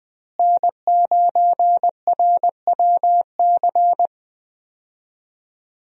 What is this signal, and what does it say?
Morse code, heard as a keyed tone, about 20 wpm, 700 Hz. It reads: N9RWC
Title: USA Call Signs - 5 Characters - Repeated 3x - 20wpm